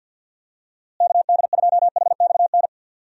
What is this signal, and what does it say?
Morse code, heard as a keyed tone, about 50 wpm, 700 Hz. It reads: KD2HXN
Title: USA Call Signs - 6 Characters - Repeated 3x - 50wpm